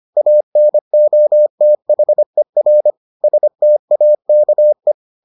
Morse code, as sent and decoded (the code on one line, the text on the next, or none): .- -. --- - .... .-. / ... - .- -.- .
ANOTHER STAKE